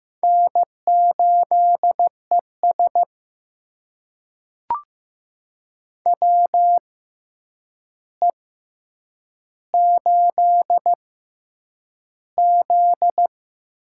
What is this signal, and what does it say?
Morse code, heard as a keyed tone, about 15 wpm, spaced as for 8 wpm, 700 Hz. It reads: N8ES WE8Z
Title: USA Call Signs - 4 Characters - 6x Character Spacing - 15wpm